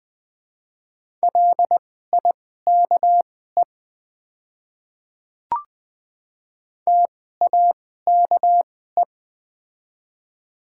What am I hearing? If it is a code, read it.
Morse code, heard as a keyed tone, about 20 wpm, spaced as for 14 wpm, 700 Hz. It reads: LIKE TAKE
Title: 2x Character Spacing - Top 200 Words - 20wpm